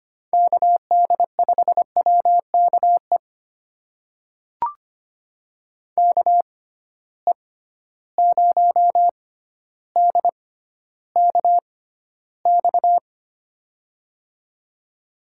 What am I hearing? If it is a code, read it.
Morse code, heard as a keyed tone, about 25 wpm, spaced as for 13 wpm, 700 Hz. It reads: KD5WKE KE0DKX
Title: USA Call Signs - 6 Characters - 6x Character Spacing - 25wpm